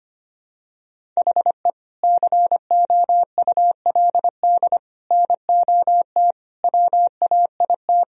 Morse code, heard as a keyed tone, about 25 wpm, 700 Hz. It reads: HE COULD NOT WAIT
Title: Sentences from Top 1000 Words - Review Part 1 - 25wpm